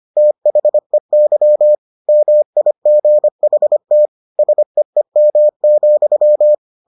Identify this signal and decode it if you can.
Morse code, heard as a keyed tone, about 25 wpm, 600 Hz. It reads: THEY MIGHT SEEM,